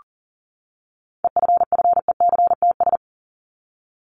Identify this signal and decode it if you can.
Morse code, heard as a keyed tone, about 40 wpm, 700 Hz. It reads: EFFECTS
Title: ES:S Suffix - Top 100 Words with Base Words 40wpm